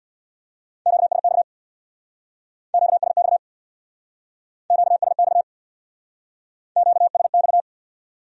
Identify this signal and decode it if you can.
Morse code, heard as a keyed tone, about 58 wpm, 700 Hz. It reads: QSX QSX QSX QSX
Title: Speed-Racing Q-Codes - 50wpm